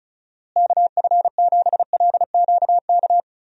Morse code, heard as a keyed tone, about 34 wpm, 700 Hz. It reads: KF7LQK
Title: USA Call Signs - 6 Characters - Repeated 3x - 35wpm